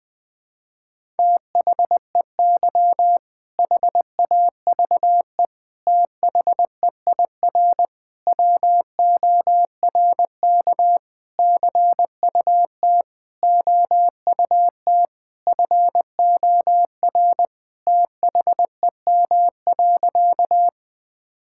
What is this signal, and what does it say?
Morse code, heard as a keyed tone, about 20 wpm, 700 Hz. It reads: THEY HAVE THEIR WORK CUT OUT FOR THEM.
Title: Sentences with 7-Word Phrases - Spoken Beginning - 20wpm